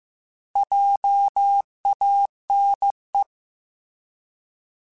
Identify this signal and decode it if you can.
Morse code, heard as a keyed tone, about 15 wpm, 800 Hz. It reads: JANE
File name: Alphabetical / Alphabetical mp3s / Js @13